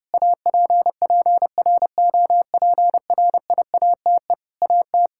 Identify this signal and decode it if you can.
Morse code, heard as a keyed tone, about 30 wpm, 700 Hz. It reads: APPROPRIATE AT
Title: Sentences from Top 1000 Words - Review Part 5 - 30wpm